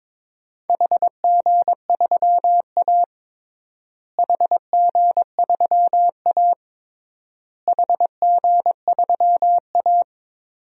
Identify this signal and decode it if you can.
Morse code, heard as a keyed tone, about 22 wpm, 700 Hz. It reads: HG3A HG3A HG3A
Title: DXCC-100 Call Signs - 4 Characters - Repeated 3x - 22wpm